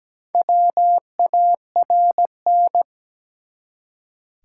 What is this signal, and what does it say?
Morse code, heard as a keyed tone, about 17 wpm, 700 Hz. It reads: WARN
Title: Four Letter Words - 17wpm